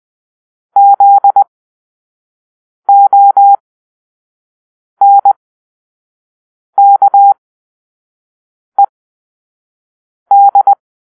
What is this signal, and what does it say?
Morse code, heard as a keyed tone, about 20 wpm, spaced as for 5 wpm, 800 Hz. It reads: ZONKED
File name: arrl / Grp-5 Drill-2